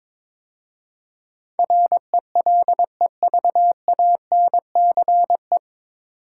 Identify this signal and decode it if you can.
Morse code, heard as a keyed tone, about 22 wpm, 700 Hz. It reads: RELEVANCE
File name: 9 Letter Words - First 3 Letters Given - 22wpm